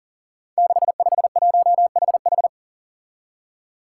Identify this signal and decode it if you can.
Morse code, heard as a keyed tone, about 40 wpm, 700 Hz. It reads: 651HH
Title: Five Challenging Letter-Number Combinations 40wpm